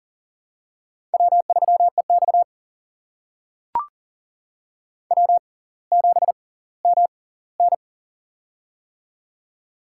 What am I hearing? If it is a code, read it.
Morse code, heard as a keyed tone, about 40 wpm, spaced as for 20 wpm, 700 Hz. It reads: W3EX W7MN
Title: USA Call Signs - 4 Characters - 6x Character Spacing - 40wpm